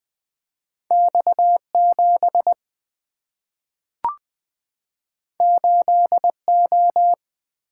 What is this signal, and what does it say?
Morse code, heard as a keyed tone, about 20 wpm, 700 Hz. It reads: X 7 8 O
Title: Two Letter-Number Combinations - 20wpm